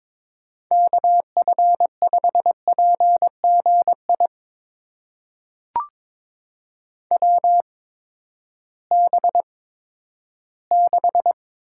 Morse code, heard as a keyed tone, about 22 wpm, spaced as for 13 wpm, 700 Hz. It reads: KF5PGI WB6
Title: USA Call Signs - 6 Characters - 8x Character Spacing - 22wpm